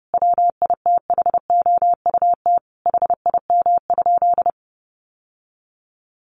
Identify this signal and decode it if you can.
Morse code, heard as a keyed tone, about 30 wpm, 700 Hz. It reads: WITHOUT HIM?
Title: Sentences from Top 1500 Words 30wpm